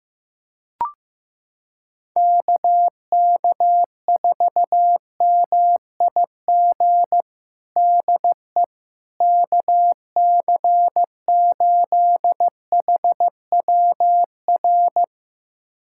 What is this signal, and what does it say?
Morse code, heard as a keyed tone, about 15 wpm, 700 Hz. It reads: KK4MIG DE KC8HWR
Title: USA Callsign DE Callsign 15wpm